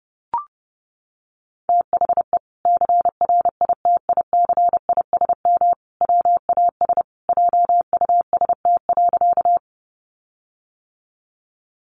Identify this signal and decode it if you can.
Morse code, heard as a keyed tone, about 30 wpm, 700 Hz. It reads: THE CRITICISM WAS JUST.
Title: Sentences from Top 1300 Words 30wpm